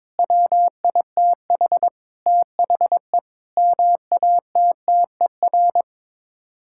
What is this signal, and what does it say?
Morse code, heard as a keyed tone, about 22 wpm, 700 Hz. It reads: WITH THE MATTER
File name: practice / 22wpm / Sets of 7 Words - 22wpm